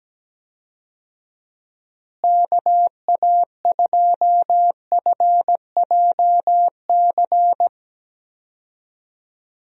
Morse code, heard as a keyed tone, about 17 wpm, 700 Hz. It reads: KA2FJC
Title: USA Call Signs - 6 Characters - Repeated 2x - 17wpm